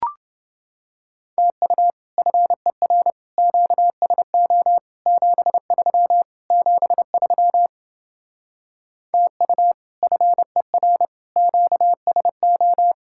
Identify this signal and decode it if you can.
Morse code, heard as a keyed tone, about 30 wpm, 700 Hz. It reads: TU FER QSO 73 73 TU FER QSO